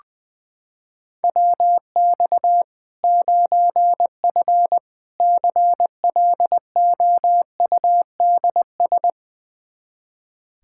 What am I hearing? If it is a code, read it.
Morse code, heard as a keyed tone, about 20 wpm, 700 Hz. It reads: WX 9F CLOUDS
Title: K1USN SST QSO Practice 20wpm